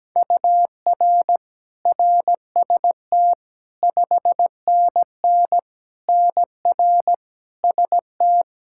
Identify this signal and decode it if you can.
Morse code, heard as a keyed tone, about 17 wpm, 700 Hz. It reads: UR RST 5NN NR ST
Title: Practice QSO - Exchange 1 - Beginning - 17wpm